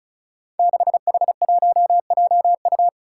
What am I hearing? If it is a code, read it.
Morse code, heard as a keyed tone, about 34 wpm, 700 Hz. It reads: 6H1JU
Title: Five Challenging Letter-Number Combinations 35wpm